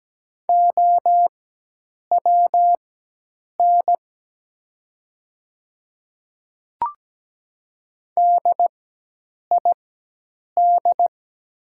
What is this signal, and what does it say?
Morse code, heard as a keyed tone, about 17 wpm, spaced as for 8 wpm, 700 Hz. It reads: OWN DID
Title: 4x Character Spacing - Top 300 Words - 17wpm